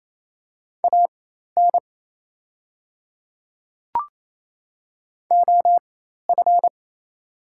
Morse code, heard as a keyed tone, about 28 wpm, spaced as for 13 wpm, 700 Hz. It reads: AN OF
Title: 4x Character Spacing - Top 300 Words - 28wpm